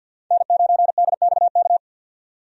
Morse code, heard as a keyed tone, about 52 wpm, 700 Hz. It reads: N9DXK
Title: USA Call Signs - 5 Characters - 50wpm